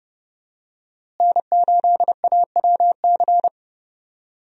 Morse code, heard as a keyed tone, about 30 wpm, 700 Hz. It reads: N8AWC
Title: USA Call Signs - 5 Characters - Repeated 2x - 30wpm